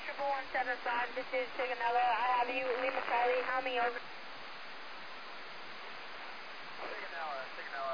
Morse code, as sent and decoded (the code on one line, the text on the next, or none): none